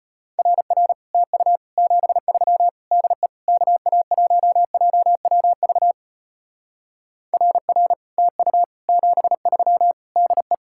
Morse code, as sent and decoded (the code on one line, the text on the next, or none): .-. .-. / - ..- / --... ...-- / -.. / -.- .- .---- .--- .-- ...- / .-. .-. / - ..- / --... ...-- / -.. .
RR TU 73 DE KA1JWV RR TU 73 DE